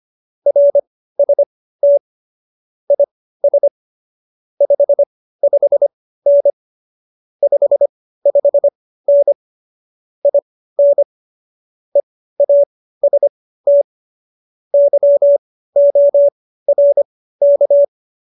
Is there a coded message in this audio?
Yes: Morse code from 0.5 to 17.8 s, heard as a keyed tone, about 25 wpm, spaced as for 15 wpm, 570 Hz. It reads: RST IS 55N 55N IN EAST YORK